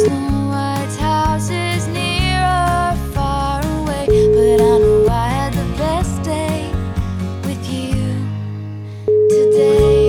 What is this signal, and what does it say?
Ringback tone (Europe, 425 Hz), with noise.